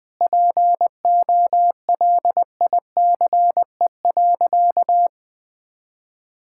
Morse code, heard as a keyed tone, about 20 wpm, 700 Hz. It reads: POLICE.
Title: Sentences with 6-Word Phrases - 20wpm